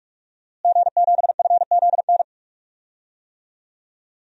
Morse code, heard as a keyed tone, about 46 wpm, 700 Hz. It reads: G7FZN
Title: DXCC-100 Call Signs - 5 Characters - Repeated 2x - 45wpm